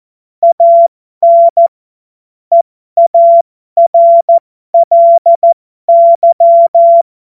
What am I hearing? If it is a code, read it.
Morse code, heard as a keyed tone, about 14 wpm, spaced as for 12 wpm, 680 Hz. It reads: AN EARLY
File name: Morsekurs / Witze